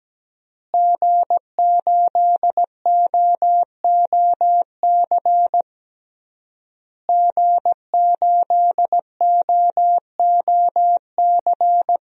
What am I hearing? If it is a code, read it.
Morse code, heard as a keyed tone, about 17 wpm, 700 Hz. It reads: G8OOC G8OOC
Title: DXCC-100 Call Signs - 5 Characters - Repeated 3x - 17wpm